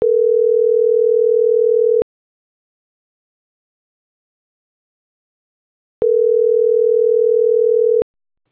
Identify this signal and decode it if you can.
Ringback tone (North America), 440 + 480 Hz.